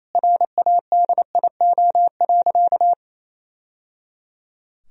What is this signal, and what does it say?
Morse code, heard as a keyed tone, about 28 wpm, 700 Hz. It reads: RADIO.
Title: Sentences from Top 1000 Words - Review Part 3 - 28wpm